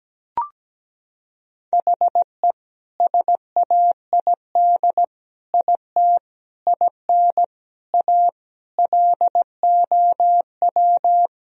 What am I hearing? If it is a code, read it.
Morse code, heard as a keyed tone, about 17 wpm, 700 Hz. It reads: HE SAID IT IN A LOW
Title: Sentences from Top 1000 Words - Review Part 1 - 17wpm